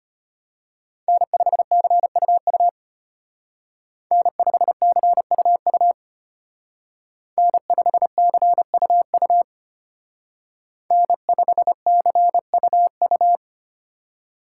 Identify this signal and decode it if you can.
Morse code, heard as a keyed tone, about 30 wpm, 700 Hz. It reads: N5CUU N5CUU N5CUU N5CUU
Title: Speed Racing - USA Call Signs 25wpm